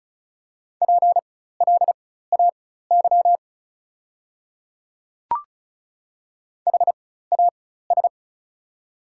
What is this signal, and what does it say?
Morse code, heard as a keyed tone, about 36 wpm, spaced as for 16 wpm, 690 Hz. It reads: PLAY HAS
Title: 4x Character Spacing - Top 200 Words - 35wpm